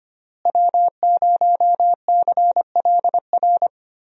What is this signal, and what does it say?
Morse code, heard as a keyed tone, about 25 wpm, 700 Hz. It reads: W0CLR
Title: USA Call Signs - 5 Characters - Repeated 3x - 25wpm